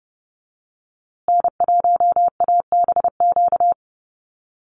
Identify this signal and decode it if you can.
Morse code, heard as a keyed tone, about 30 wpm, 700 Hz. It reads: N1ABQ
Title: USA Call Signs Repeated 3x 30wpm